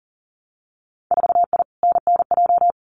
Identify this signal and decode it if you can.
Morse code, heard as a keyed tone, about 40 wpm, 700 Hz. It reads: 4I NNJ
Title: Winter Field Day Exchanges 40wpm